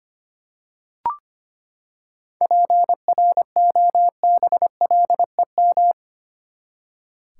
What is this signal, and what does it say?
Morse code, heard as a keyed tone, about 25 wpm, 700 Hz. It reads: PROBLEM
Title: Seven Letter Words - 25wpm